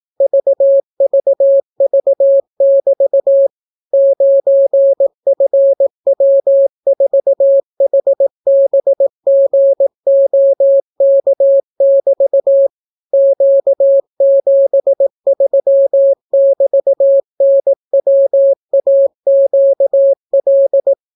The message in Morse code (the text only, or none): VVV<BT> 9FW4HBGOK<BT> Q73<BT>NWAQL